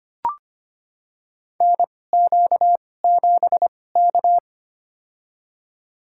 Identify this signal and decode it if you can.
Morse code, heard as a keyed tone, about 25 wpm, spaced as for 18 wpm, 700 Hz. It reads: NQ7K
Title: USA Call Signs - 4 Characters - 2x Character Spacing - 25wpm